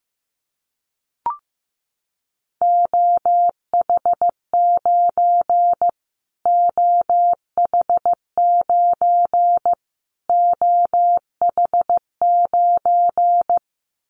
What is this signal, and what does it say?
Morse code, heard as a keyed tone, about 14 wpm, 700 Hz. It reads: OH9 OH9 OH9
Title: Callsign Prefixes for Top 25 DXCC - Repeated 3x - 15wpm